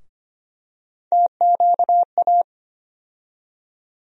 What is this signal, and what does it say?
Morse code, heard as a keyed tone, about 25 wpm, 700 Hz. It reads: TQA
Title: USA Call Sign Postfixes - 3 Letters 25wpm